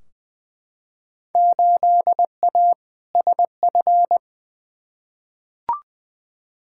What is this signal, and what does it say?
Morse code, heard as a keyed tone, about 20 wpm, 700 Hz. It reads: 8A SF
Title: Field Day Exchanges Encore - 20wpm